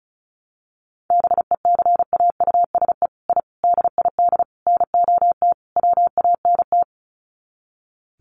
Morse code, heard as a keyed tone, about 34 wpm, 700 Hz. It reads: BECAUSE I DID NOT WANT
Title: Sets of 5 Words Encore2 35wpm